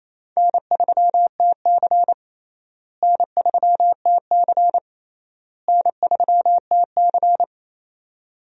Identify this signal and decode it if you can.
Morse code, heard as a keyed tone, about 28 wpm, 700 Hz. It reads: N3TC N3TC N3TC
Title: USA Call Signs - 4 Characters - Repeated 3x - 28wpm